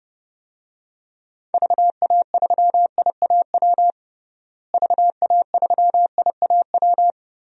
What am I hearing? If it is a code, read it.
Morse code, heard as a keyed tone, about 32 wpm, 700 Hz. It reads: VA3IAW VA3IAW
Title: DXCC-100 Call Signs - 6 Characters - Repeated 3x - 30wpm